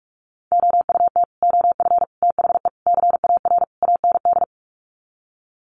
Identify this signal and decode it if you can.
Morse code, heard as a keyed tone, about 44 wpm, 700 Hz. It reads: OUT OF THE CAR AND
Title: Sets of 5 Words Encore 45wpm